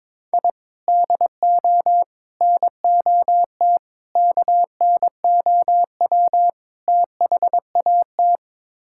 Morse code, heard as a keyed tone, about 22 wpm, 700 Hz. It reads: I DO NOT KNOW THAT
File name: practice / 22wpm / Build A Phrase - 5-Word Phrases - 22wpm